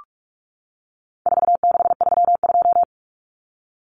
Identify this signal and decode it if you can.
Morse code, heard as a keyed tone, about 44 wpm, 700 Hz. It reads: 4632
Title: Common Cut Numbers - 4-Digit 45wpm